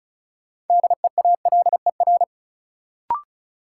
Morse code, heard as a keyed tone, about 36 wpm, 700 Hz. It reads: DEALER
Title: Sentences from Top 1100 Words - New Word Drill - 35wpm